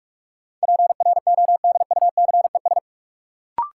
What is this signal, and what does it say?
Morse code, heard as a keyed tone, about 46 wpm, 700 Hz. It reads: PRODUCES
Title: Sentences from Top 1000 Words - Review Part 6 - New Word Drill - 45wpm